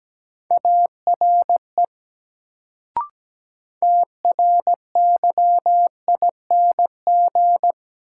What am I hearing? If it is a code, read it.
Morse code, heard as a keyed tone, about 17 wpm, 700 Hz. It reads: ARE TRYING
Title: Sets of 2 Words - New Word Drill - 17wpm